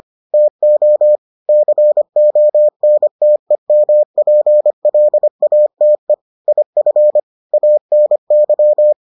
Morse code, heard as a keyed tone, about 25 wpm, 600 Hz. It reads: TO CONTEMPLATE IF ANY